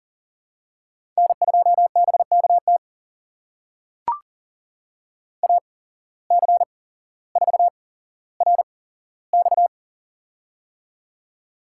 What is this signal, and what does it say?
Morse code, heard as a keyed tone, about 40 wpm, spaced as for 17 wpm, 700 Hz. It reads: N2BKT AC4RX
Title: USA Call Signs - 5 Characters - 8x Character Spacing - 40wpm